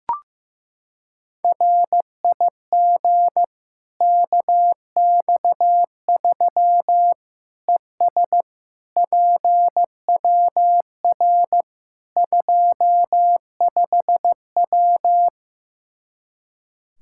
Morse code, heard as a keyed tone, about 16 wpm, 700 Hz. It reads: RIG KX3 ES PWR 25W
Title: Practice QSO - Exchange 2 - Info - 15wpm